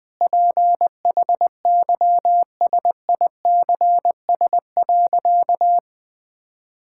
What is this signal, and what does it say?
Morse code, heard as a keyed tone, about 20 wpm, 700 Hz. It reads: PHYSICS.
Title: Sentences from Top 1000 Words - Review Part 5 - 20wpm